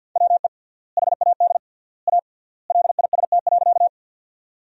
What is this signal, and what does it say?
Morse code, heard as a keyed tone, about 50 wpm, 700 Hz. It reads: WE HAD A LIST.
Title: 3x Word-Spacing - Sentences from Top 400 Words - 50wpm